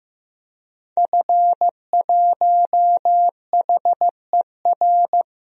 Morse code, heard as a keyed tone, about 15 wpm, 700 Hz. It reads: F1HER
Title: DXCC-100 Call Signs - 5 Characters - 15wpm